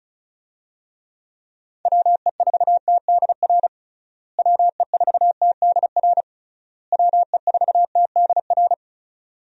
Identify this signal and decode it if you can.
Morse code, heard as a keyed tone, about 36 wpm, 700 Hz. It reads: WE4TDR WE4TDR WE4TDR
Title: USA Call Signs - 6 Characters - Repeated 3x - 35wpm